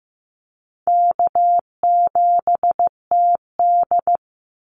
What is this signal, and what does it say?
Morse code, heard as a keyed tone, about 15 wpm, 700 Hz. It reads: K7TD
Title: Easy Call Signs Repeated 3x - 15wpm